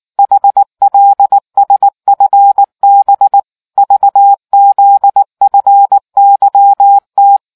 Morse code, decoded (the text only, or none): HLSFB VZFYT